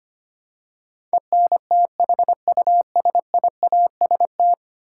Morse code, heard as a keyed tone, about 25 wpm, 700 Hz. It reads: ENTHUSIAST
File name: Ten Letter Words - 25wpm